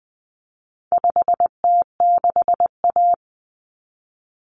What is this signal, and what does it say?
Morse code, heard as a keyed tone, about 20 wpm, 700 Hz. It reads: 5T6A
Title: Common Cut Numbers - 4-Digit 20wpm